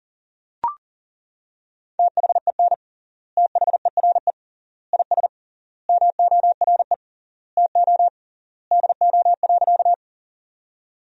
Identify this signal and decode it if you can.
Morse code, heard as a keyed tone, about 40 wpm, 700 Hz. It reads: THEN THERE IS MORE TO DO.
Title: 3x Word-Spacing - Sentences from Top 100 Words - 40wpm